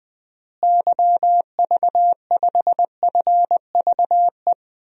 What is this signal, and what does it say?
Morse code, heard as a keyed tone, about 20 wpm, 700 Hz. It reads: YV5FVE
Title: DXCC-100 Call Signs - 6 Characters - Repeated 3x - 20wpm